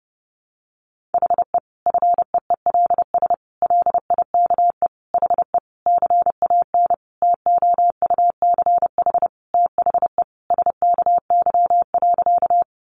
Morse code, heard as a keyed tone, about 28 wpm, 700 Hz. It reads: HE FEELS LIKE HE CAN TOUCH THE SKY.
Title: Additional Easy Phrases 30wpm